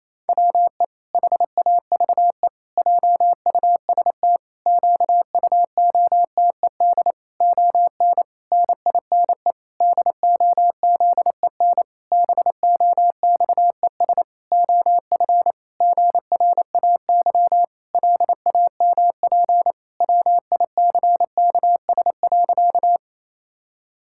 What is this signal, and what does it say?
Morse code, heard as a keyed tone, about 27 wpm, 700 Hz. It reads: WE HAVE JUST QUOTED ON NINE DOZEN BOXES OF GRAY LAMP WICKS.
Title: Pangram Challenge v2 - 28wpm